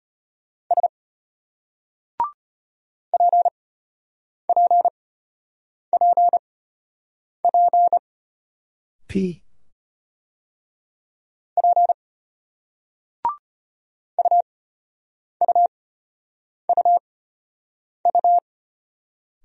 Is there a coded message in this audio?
Yes: Morse code from 0.7 to 18.4 s, heard as a keyed tone, about 32 wpm, spaced as for 8 wpm, 700 Hz. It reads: S PPPP P UUUU